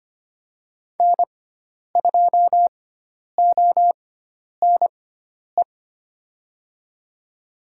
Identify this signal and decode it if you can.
Morse code, heard as a keyed tone, about 25 wpm, spaced as for 10 wpm, 700 Hz. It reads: N2ONE
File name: Lesson 16g - Letter c - Call Signs 25wpm